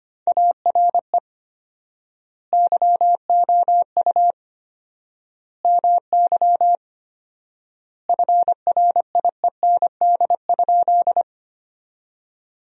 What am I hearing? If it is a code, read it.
Morse code, heard as a keyed tone, about 25 wpm, 700 Hz. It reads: ARE YOU MY FRIEND?
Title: 4x Word-Spacing - Sentences from Top 300 Words - 25wpm